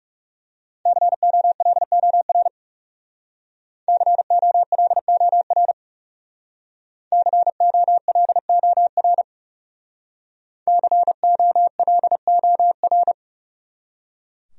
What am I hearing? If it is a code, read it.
Morse code, heard as a keyed tone, about 37 wpm, 700 Hz. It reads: COLOR COLOR COLOR COLOR
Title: Speed-Racing Top 300 Words 30wpm